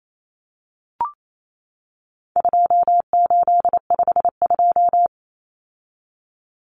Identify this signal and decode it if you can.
Morse code, heard as a keyed tone, about 28 wpm, 700 Hz. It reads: 2852
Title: Common Cut Numbers - 4-Digit 28wpm